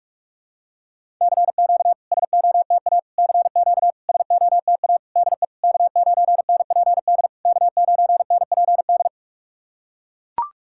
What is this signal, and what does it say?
Morse code, heard as a keyed tone, about 44 wpm, 690 Hz. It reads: CQ SOTA CQ SOTA DE K9NPD K9NPD
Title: SOTA QSO Practice 45wpm